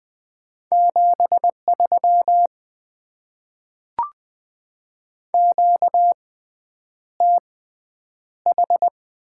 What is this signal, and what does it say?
Morse code, heard as a keyed tone, about 20 wpm, 700 Hz. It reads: 7 3 Q T H
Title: Top 20 QSO Elements - 6x Character Spacing - 20wpm